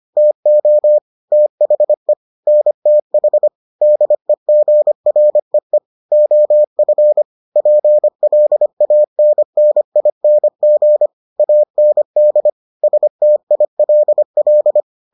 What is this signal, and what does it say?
Morse code, heard as a keyed tone, about 25 wpm, 600 Hz. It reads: TO THE NTH DEGREE OF PLANNING AND STILL